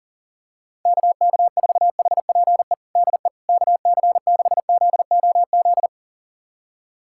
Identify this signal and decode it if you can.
Morse code, heard as a keyed tone, about 40 wpm, 700 Hz. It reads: KK4HPE DE KC6ZOZ
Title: USA Callsign DE Callsign - 6 Characters - 40wpm